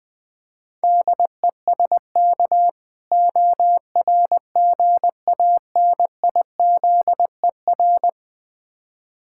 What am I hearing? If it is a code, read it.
Morse code, heard as a keyed tone, about 20 wpm, 700 Hz. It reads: DESK ORGANIZER
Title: Difficult Words with Hints - 20wpm